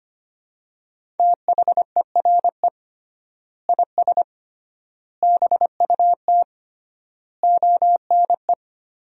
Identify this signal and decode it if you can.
Morse code, heard as a keyed tone, about 25 wpm, 700 Hz. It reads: THERE IS BUT ONE
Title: 3x Word-Spacing - Sentences from Top 100 Words - 25wpm